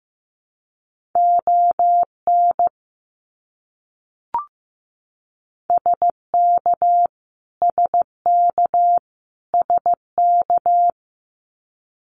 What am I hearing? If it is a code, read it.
Morse code, heard as a keyed tone, about 15 wpm, 700 Hz. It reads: ON SK SK SK
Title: Canadian Provinces and Territories - Repeated 3x - 15wpm